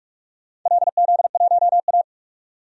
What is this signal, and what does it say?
Morse code, heard as a keyed tone, about 46 wpm, 700 Hz. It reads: LZ1A
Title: DXCC-100 Call Signs - 4 Characters - 45wpm